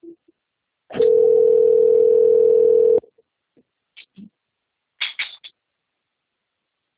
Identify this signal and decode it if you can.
Ringback tone (North America), 440 + 480 Hz.